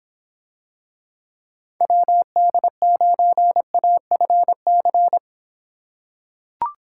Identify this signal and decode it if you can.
Morse code, heard as a keyed tone, about 25 wpm, 700 Hz. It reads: WD9AFC